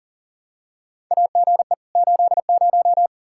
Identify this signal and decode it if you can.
Morse code, heard as a keyed tone, about 38 wpm, 700 Hz. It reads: AGE 80
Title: K1USN SST QSO Practice 40wpm